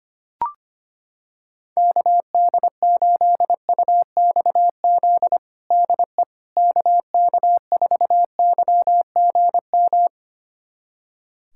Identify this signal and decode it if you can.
Morse code, heard as a keyed tone, about 25 wpm, 700 Hz. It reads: KD8UXZ DE KK4YGM
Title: USA Callsign DE Callsign - 6 Characters - 25wpm